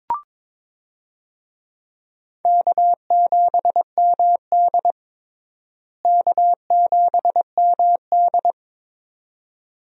Morse code, heard as a keyed tone, about 22 wpm, 700 Hz. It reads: K7MD K7MD
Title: USA Call Signs - 4 Characters - Repeated 2x - 22wpm